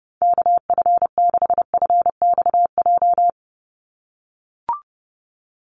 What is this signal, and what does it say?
Morse code, heard as a keyed tone, about 30 wpm, 700 Hz. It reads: KF6FXJ
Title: USA Call Signs 30wpm